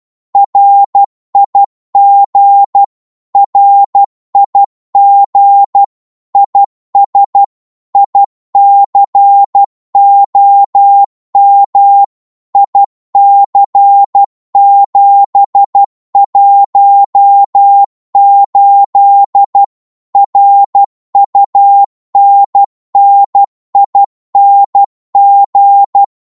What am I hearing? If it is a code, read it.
Morse code, heard as a keyed tone, about 12 wpm, 800 Hz. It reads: RIG RIG IS ICOM IC718 RUNNING
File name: Morse / qso-12wpm